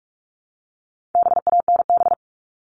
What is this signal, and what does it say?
Morse code, heard as a keyed tone, about 44 wpm, 700 Hz. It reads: BANB
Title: All Cut Numbers - 4-Digit 45wpm